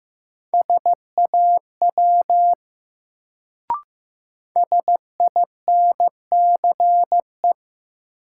Morse code, heard as a keyed tone, about 15 wpm, 700 Hz. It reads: SAW SINCE
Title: Sentences from Top 300 Words - New Word Drill - 15wpm